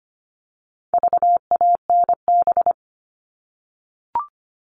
Morse code, heard as a keyed tone, about 25 wpm, 700 Hz. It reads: VANB
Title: All Cut Numbers - 4-Digit 25wpm